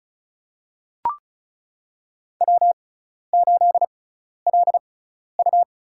Morse code, heard as a keyed tone, about 34 wpm, spaced as for 12 wpm, 700 Hz. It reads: W8LU
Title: USA Call Signs - 4 Characters - 6x Character Spacing - 35wpm